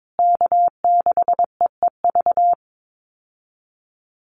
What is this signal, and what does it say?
Morse code, heard as a keyed tone, about 22 wpm, 700 Hz. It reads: K6EEV
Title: USA Call Signs Repeated 2x 22wpm